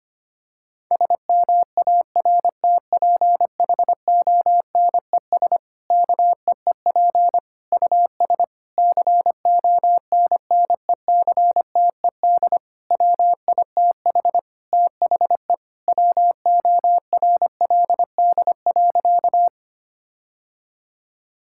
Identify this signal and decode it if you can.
Morse code, heard as a keyed tone, about 25 wpm, 700 Hz. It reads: SMARTPHONES KEEP US CONNECTED WITH THE WORLD.
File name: practice / 25wpm / Difficult Sentences with Hints - 25wpm